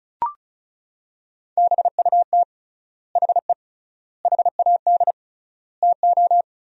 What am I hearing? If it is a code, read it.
Morse code, heard as a keyed tone, about 36 wpm, 700 Hz. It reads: BUT HE HAD TO
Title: 3x Word-Spacing - Sentences from Top 100 Words - 35wpm